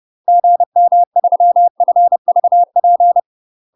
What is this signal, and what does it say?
Morse code, heard as a keyed tone, about 30 wpm, 700 Hz. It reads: GM3FVP
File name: UK MASTER QSOs TRAINING / 30WPM / UK MASTER QSOs 0041